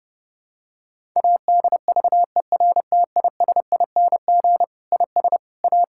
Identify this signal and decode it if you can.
Morse code, heard as a keyed tone, about 32 wpm, 700 Hz. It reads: ADVERTISING IS A
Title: Sentences from Top 1000 Words - Review Part 6 - 30wpm